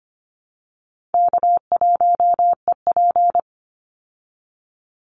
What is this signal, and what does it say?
Morse code, heard as a keyed tone, about 25 wpm, 700 Hz. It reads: K1EP
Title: Easy Call Signs Repeated 3x - 25wpm